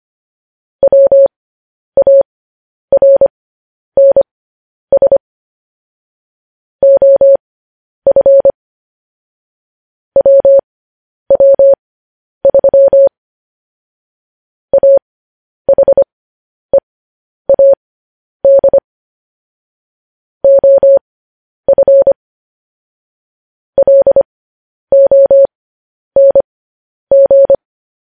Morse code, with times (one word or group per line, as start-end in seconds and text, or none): 0.8-5.2 WARNS
6.8-8.5 OF
10.2-13.1 WW3
14.7-18.8 AHEAD
20.4-22.1 OF
23.8-27.6 LONG